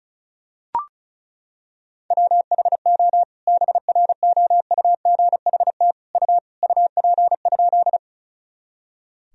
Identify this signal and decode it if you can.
Morse code, heard as a keyed tone, about 36 wpm, 700 Hz. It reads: WHO BROUGHT U UP?
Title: Sentences from Top 500 Words and Abbreviations 35wpm